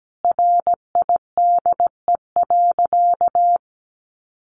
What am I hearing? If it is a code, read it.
Morse code, heard as a keyed tone, about 17 wpm, 700 Hz. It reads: RIDE.
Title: Sentences from Top 1200 Words 17wpm